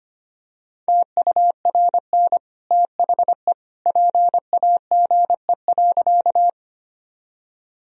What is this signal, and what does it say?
Morse code, heard as a keyed tone, about 25 wpm, 700 Hz. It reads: TURN THE PAGE.
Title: Sentences from Top 1000 Words - Review Part 1 - 25wpm